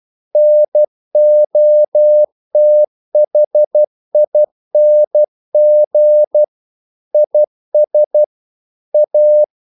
Morse code, heard as a keyed tone, about 12 wpm, 600 Hz. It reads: NOTHING IS A
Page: English quotations in CW (telegraphy - morse)